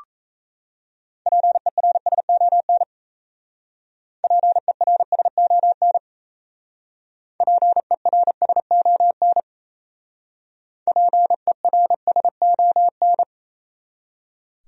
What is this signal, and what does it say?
Morse code, heard as a keyed tone, about 35 wpm, 700 Hz. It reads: PERSON PERSON PERSON PERSON